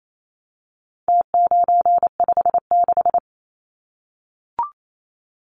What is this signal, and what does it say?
Morse code, heard as a keyed tone, about 28 wpm, 700 Hz. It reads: T956
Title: Numbers - Contest Style - 4-Digit 28wpm